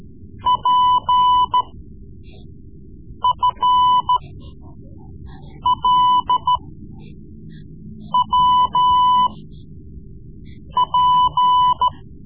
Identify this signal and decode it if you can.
Morse code, heard as a keyed tone, about 14 wpm, spaced as for 5 wpm, 1000 Hz. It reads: PFLWP
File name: Morse Lessons / Morse Lessons / Tape-3B